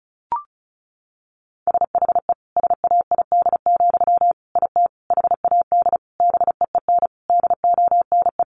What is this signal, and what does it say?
Morse code, heard as a keyed tone, about 34 wpm, 700 Hz. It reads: SHE SAID, IT HAD BEEN DONE
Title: Sentences from Top 800 Words 35wpm